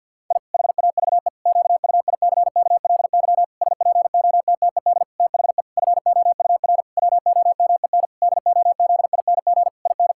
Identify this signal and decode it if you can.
Morse code, heard as a keyed tone, about 51 wpm, 700 Hz. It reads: I HAVE QUICKLY SPOTTED THE FOUR WOMEN DOZING IN
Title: Pangram Challenge v2 - 50wpm